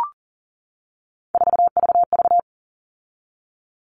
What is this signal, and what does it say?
Morse code, heard as a keyed tone, about 40 wpm, 700 Hz. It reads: 4VV
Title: All Cut Numbers - 3-Digit 40wpm